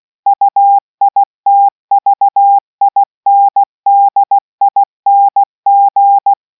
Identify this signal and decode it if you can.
Morse code, heard as a keyed tone, about 16 wpm, 800 Hz. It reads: UITVINDING